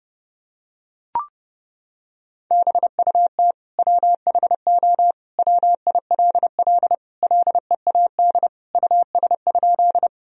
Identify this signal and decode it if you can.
Morse code, heard as a keyed tone, about 30 wpm, 700 Hz. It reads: BUT WHO WILL LEAD US?
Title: Sentences from Top 1000 Words - Review Part 2 - 30wpm